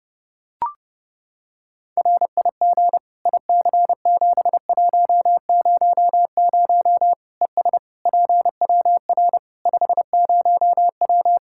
Morse code, heard as a keyed tone, about 29 wpm, 700 Hz. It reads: RIG IC7100 ES PWR 50W
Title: Practice QSO - Exchange 2 - Info - 30wpm